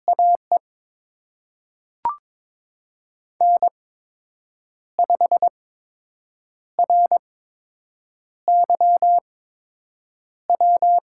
Morse code, heard as a keyed tone, about 22 wpm, spaced as for 7 wpm, 700 Hz. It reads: AE N5RYW
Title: USA Call Signs - 5 Characters - 8x Character Spacing - 22wpm